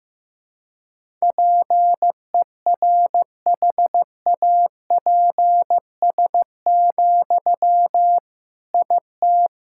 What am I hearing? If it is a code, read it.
Morse code, heard as a keyed tone, about 15 wpm, 700 Hz. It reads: PERHAPS, IT
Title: Sentences from Top 1000 Words - Review Part 6 - 15wpm